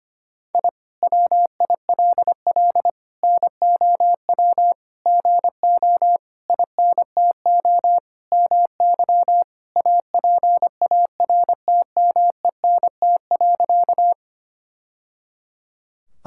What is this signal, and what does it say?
Morse code, heard as a keyed tone, about 25 wpm, 700 Hz. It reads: I WILL NOW GO INTO MY APARTMENT.
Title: Sentences from Top 1000 Words - Review Part 6 - 25wpm